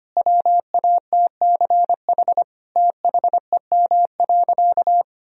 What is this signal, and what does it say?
Morse code, heard as a keyed tone, about 25 wpm, 700 Hz. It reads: WATCH THEM.
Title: Sentences from Top 1000 Words - Review Part 1 - 25wpm